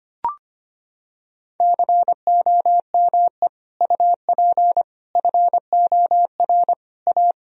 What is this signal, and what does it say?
Morse code, heard as a keyed tone, about 25 wpm, 700 Hz. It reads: COME UP FOR A
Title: Sentences from Top 1000 Words - Review Part 2 - 25wpm